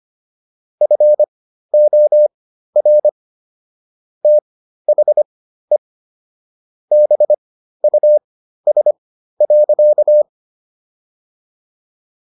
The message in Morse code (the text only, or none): FOR THE BUS.